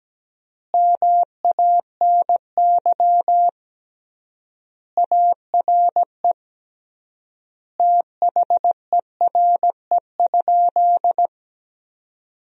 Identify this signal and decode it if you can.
Morse code, heard as a keyed tone, about 17 wpm, 700 Hz. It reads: MANY ARE THERE?
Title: 3x Word-Spacing - Sentences from Top 100 Words - 17wpm